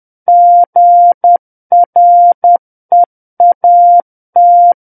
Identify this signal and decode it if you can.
Morse code, heard as a keyed tone, about 10 wpm, 700 Hz. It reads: GREAT
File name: Book One / 01-Chapter 1 - 10 WPM